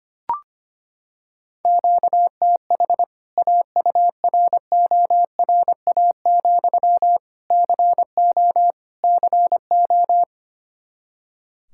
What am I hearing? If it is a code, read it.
Morse code, heard as a keyed tone, about 25 wpm, 700 Hz. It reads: QTH AURORA, CO CO
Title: K1USN SST QSO Practice 25wpm